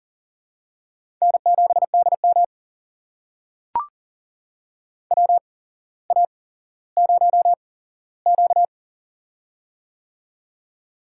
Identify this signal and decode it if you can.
Morse code, heard as a keyed tone, about 43 wpm, spaced as for 17 wpm, 700 Hz. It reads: N7DM WA0Q
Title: USA Call Signs - 4 Characters - 8x Character Spacing - 40wpm